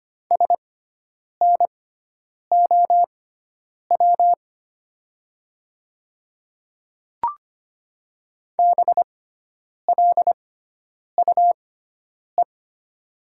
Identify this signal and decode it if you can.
Morse code, heard as a keyed tone, about 25 wpm, spaced as for 9 wpm, 700 Hz. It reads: SNOW BLUE
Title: 6x Character Spacing - Top 500 Words - 25wpm